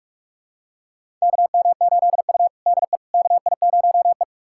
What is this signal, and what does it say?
Morse code, heard as a keyed tone, about 48 wpm, 700 Hz. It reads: KM8U DE KI0E
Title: USA Callsign DE Callsign - 4 Characters - 45wpm